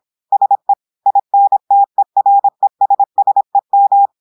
Morse code, heard as a keyed tone, about 26 wpm, 800 Hz. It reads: SE INTERESSEM
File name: portugais texte en clair 2 vitesse 26 mots minutes